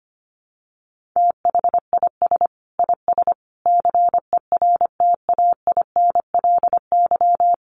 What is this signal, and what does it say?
Morse code, heard as a keyed tone, about 25 wpm, 700 Hz. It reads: THIS IS CERTAINLY
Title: Sentences from Top 1300 Words 25wpm